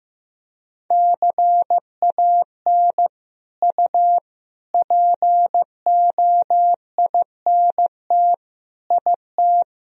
Morse code, heard as a keyed tone, about 15 wpm, 700 Hz. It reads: CAN U POINT IT
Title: Sentences from Top 200 Words and Abbreviations 15wpm